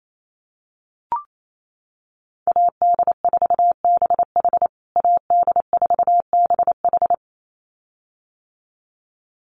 Morse code, heard as a keyed tone, about 28 wpm, 700 Hz. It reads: AD4BH AD4BH
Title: USA Call Signs Repeated 2x 28wpm